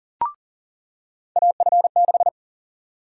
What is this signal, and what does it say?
Morse code, heard as a keyed tone, about 40 wpm, 700 Hz. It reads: AF6
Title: USA Call Sign Prefixes - 40wpm